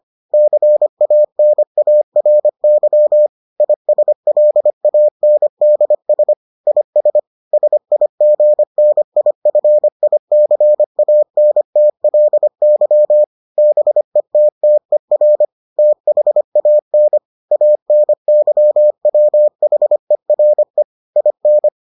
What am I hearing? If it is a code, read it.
Morse code, heard as a keyed tone, about 25 wpm, 600 Hz. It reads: CANARY ISLANDS IS SIGNIFICANTLY BETTER THAN ANYWHERE IN